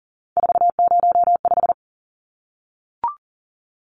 Morse code, heard as a keyed tone, about 40 wpm, 700 Hz. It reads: 405
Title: Numbers - 3-Digit 40wpm